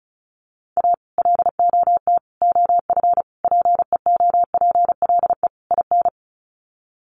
Morse code, heard as a keyed tone, about 34 wpm, 700 Hz. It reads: A LOT OF PEOPLE IN
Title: Sets of 5 Words 35wpm